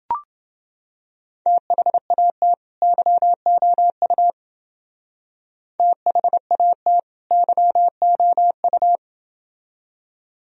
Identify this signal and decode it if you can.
Morse code, heard as a keyed tone, about 28 wpm, 700 Hz. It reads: THAT YOU THAT YOU